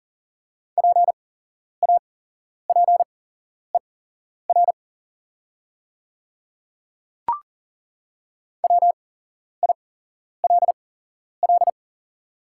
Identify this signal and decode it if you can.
Morse code, heard as a keyed tone, about 40 wpm, spaced as for 11 wpm, 700 Hz. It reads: PAPER WILL
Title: 8x Character Spacing - Top 500 Words - 40wpm